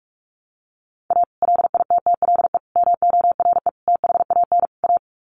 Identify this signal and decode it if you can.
Morse code, heard as a keyed tone, about 44 wpm, 700 Hz. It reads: A LITTLE MORE THAN A
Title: Sets of 5 Words Encore2 45wpm